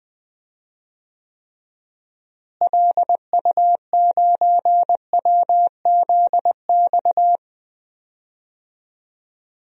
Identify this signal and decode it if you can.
Morse code, heard as a keyed tone, about 21 wpm, 700 Hz. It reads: LU9WZX
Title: DXCC-100 Call Signs - 6 Characters - Repeated 3x - 20wpm